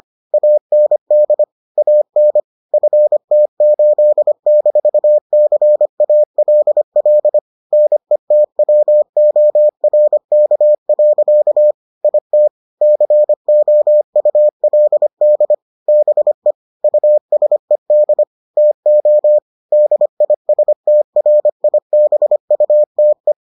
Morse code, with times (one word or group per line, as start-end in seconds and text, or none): 0.3-1.4 AND
1.8-2.4 AN
2.7-7.4 FT8-CALL
7.7-11.7 NETWORK.
12.0-12.5 IT
12.8-15.5 COULD
15.9-16.5 BE
16.8-18.2 USED
18.6-19.4 TO
19.7-23.3 DISTRIBUTE